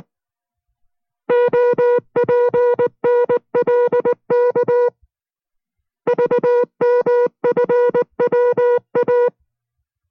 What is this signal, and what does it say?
Morse code, heard as a keyed tone, about 19 wpm, 490 Hz. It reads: OPNLK VMFWA